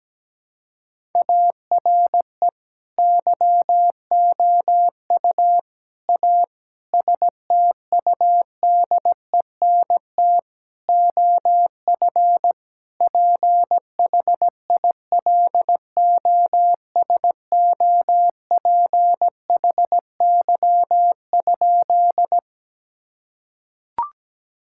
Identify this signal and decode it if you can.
Morse code, heard as a keyed tone, about 17 wpm, 700 Hz. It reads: ARE YOU A STUDENT OF PHILOSOPHY?
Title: Sentences from Top 1000 Words - Review Part 6 - 17wpm